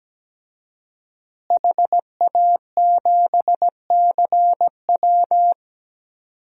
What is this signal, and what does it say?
Morse code, heard as a keyed tone, about 17 wpm, 700 Hz. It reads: HA7CW
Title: DXCC-100 Call Signs - 5 Characters - Repeated 3x - 17wpm